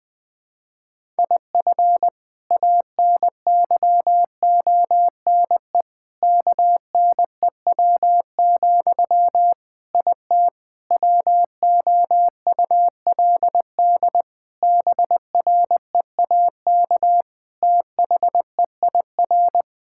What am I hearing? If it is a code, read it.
Morse code, heard as a keyed tone, about 20 wpm, 700 Hz. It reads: IF ANYONE KNEW, IT WOULD BREAK THEIR
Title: Sentences from Top 1000 Words - Review Part 6 - 20wpm